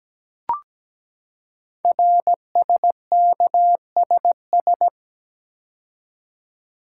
Morse code, heard as a keyed tone, about 17 wpm, 700 Hz. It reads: RSKSS
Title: Code Groups - 5 Letters - 17wpm